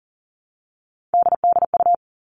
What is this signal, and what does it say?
Morse code, heard as a keyed tone, about 40 wpm, 700 Hz. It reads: DDU
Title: All Cut Numbers - 3-Digit 40wpm